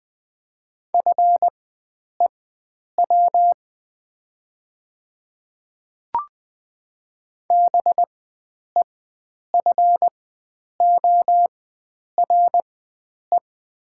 Morse code, heard as a keyed tone, about 20 wpm, spaced as for 9 wpm, 700 Hz. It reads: FEW BEFORE
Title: 4x Character Spacing - Top 300 Words - 20wpm